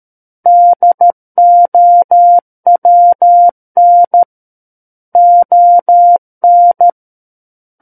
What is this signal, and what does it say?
Morse code, heard as a keyed tone, about 13 wpm, 700 Hz. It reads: DOWN ON
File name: Book One / 04-Chapter 4 - 13 WPM